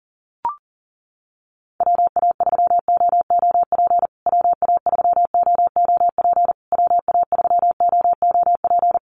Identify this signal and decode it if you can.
Morse code, heard as a keyed tone, about 37 wpm, 700 Hz. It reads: WA3OOP WA3OOP WA3OOP
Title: USA Call Signs Repeated 3x 40wpm